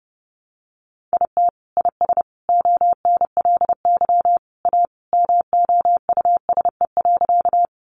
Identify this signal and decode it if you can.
Morse code, heard as a keyed tone, about 30 wpm, 700 Hz. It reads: IT IS ONLY A MOUSE.
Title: Sentences from Top 700 Words 30wpm